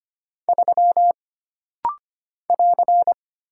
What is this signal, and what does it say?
Morse code, heard as a keyed tone, about 25 wpm, 700 Hz. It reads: 3 <AR>
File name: Single Character (Letter:Number:Punctuation:Prosign) - Rapid-Fire - 25wpm